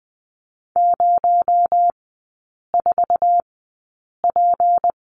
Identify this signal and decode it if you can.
Morse code, heard as a keyed tone, about 20 wpm, spaced as for 8 wpm, 700 Hz. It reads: 04P
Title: 218 Lesson 35e - Number 0 - Three Characters 20wpm